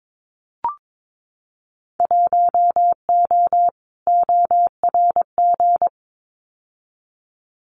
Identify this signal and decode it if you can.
Morse code, heard as a keyed tone, about 22 wpm, 700 Hz. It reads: 1O ORG
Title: Winter Field Day Exchanges 22wpm